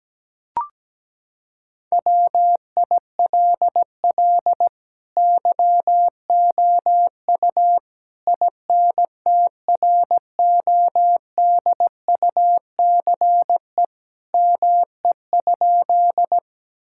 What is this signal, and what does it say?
Morse code, heard as a keyed tone, about 17 wpm, 700 Hz. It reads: WILL YOU INTRODUCE ME?
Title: Sentences from Top 1600 Words 17wpm